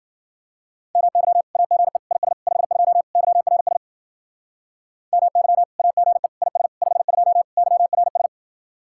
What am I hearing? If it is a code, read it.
Morse code, heard as a keyed tone, about 49 wpm, 700 Hz. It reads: MY AGE IS 52 YRS MY AGE IS 52 YRS
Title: Speed-Racing Common QSO Phrases Encore - 40wpm